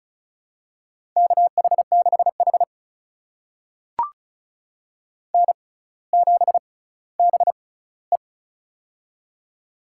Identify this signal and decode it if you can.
Morse code, heard as a keyed tone, about 36 wpm, spaced as for 18 wpm, 700 Hz. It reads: KH6H N7BE
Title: USA Call Signs - 4 Characters - 6x Character Spacing - 35wpm